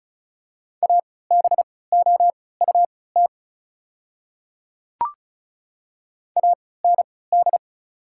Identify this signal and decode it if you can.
Morse code, heard as a keyed tone, about 36 wpm, spaced as for 20 wpm, 700 Hz. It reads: ABOUT AND
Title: 3x Character Spacing - Top 200 Words - 35wpm